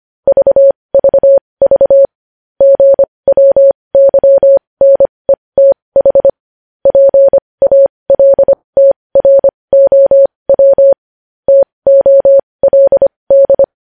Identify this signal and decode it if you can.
Morse code, heard as a keyed tone, about 25 wpm, spaced as for 20 wpm, 550 Hz. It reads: VVV GWYNETH PALTROW TOLD
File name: podcast / News headlines-1x-25x20